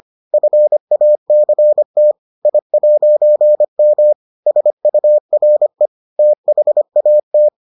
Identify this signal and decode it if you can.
Morse code, heard as a keyed tone, about 25 wpm, 600 Hz. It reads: FACT I'M SURE THAT